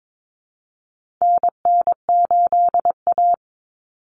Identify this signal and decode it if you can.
Morse code, heard as a keyed tone, about 22 wpm, 700 Hz. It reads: NN8A
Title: Common Cut Numbers - 4-Digit 22wpm